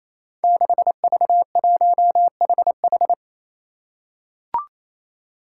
Morse code, heard as a keyed tone, about 28 wpm, 700 Hz. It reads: 6V1HH
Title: Five Challenging Letter-Number Combinations 28wpm